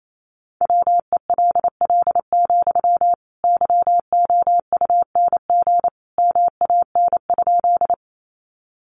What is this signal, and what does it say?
Morse code, heard as a keyed tone, about 28 wpm, 700 Hz. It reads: WELL, YOUNG MAN?
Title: Sentences from Top 400 Words 28wpm